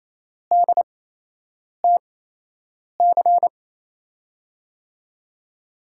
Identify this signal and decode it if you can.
Morse code, heard as a keyed tone, about 28 wpm, spaced as for 8 wpm, 700 Hz. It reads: DTC
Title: USA Call Signs - 5 Characters - 8x Character Spacing - 28wpm